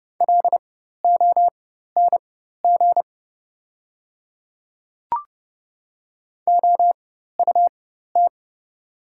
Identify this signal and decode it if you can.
Morse code, heard as a keyed tone, about 30 wpm, spaced as for 14 wpm, 700 Hz. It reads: LONG OUT
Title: 4x Character Spacing - Top 300 Words - 30wpm